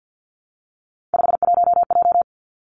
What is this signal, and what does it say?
Morse code, heard as a keyed tone, about 50 wpm, 700 Hz. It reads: 51J
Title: Three Challenging Letter-Number Combinations 50wpm